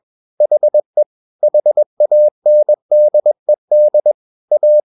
Morse code, heard as a keyed tone, about 21 wpm, 600 Hz. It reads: HE HANDED A
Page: CW med Gnister